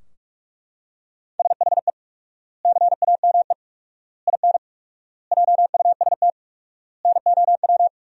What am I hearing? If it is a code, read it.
Morse code, heard as a keyed tone, about 42 wpm, 700 Hz. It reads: SHE CAME IN JUST NOW
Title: 4x Word-Spacing - Sentences from Top 300 Words - 45wpm